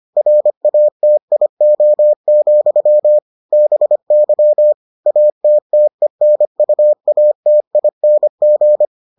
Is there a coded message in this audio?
Yes: Morse code from 0.2 to 8.9 s, heard as a keyed tone, about 25 wpm, 600 Hz. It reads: RATIO, BY ATTENUATING